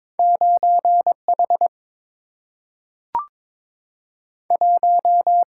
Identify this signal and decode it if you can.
Morse code, heard as a keyed tone, about 22 wpm, 700 Hz. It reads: 9H 1
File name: practice / 22wpm / Two Letter-Number Combinations - 22wpm